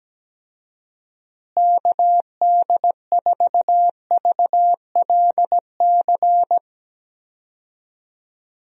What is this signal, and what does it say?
Morse code, heard as a keyed tone, about 17 wpm, 700 Hz. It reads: KD4VLC
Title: USA Call Signs - 6 Characters - Repeated 2x - 17wpm